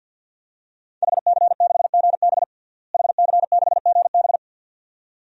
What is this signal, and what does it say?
Morse code, heard as a keyed tone, about 52 wpm, 700 Hz. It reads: HC6GB HC6GB
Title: DXCC-100 Call Signs - 5 Characters - Repeated 2x - 50wpm